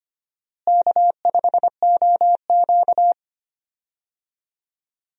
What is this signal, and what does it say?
Morse code, heard as a keyed tone, about 25 wpm, 700 Hz. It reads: K5OQ
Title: USA Call Signs - 4 Characters - Repeated 2x - 25wpm